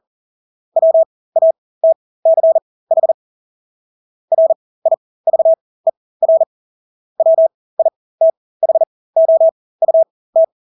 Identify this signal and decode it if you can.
Morse code, heard as a keyed tone, about 40 wpm, spaced as for 20 wpm, 650 Hz. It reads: WATCH RIVER WITHOUT
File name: Engelske ord 40-20 wpm